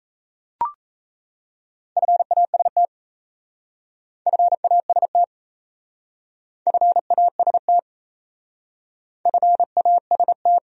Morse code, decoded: FAST FAST FAST FAST

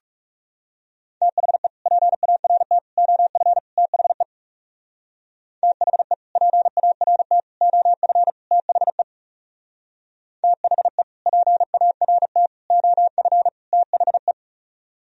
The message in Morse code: THE PART OF THE THE PART OF THE THE PART OF THE

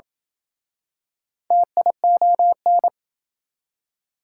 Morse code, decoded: TION

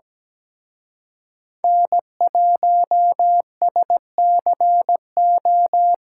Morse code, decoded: N1SCO

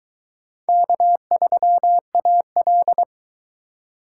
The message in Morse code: K3AL

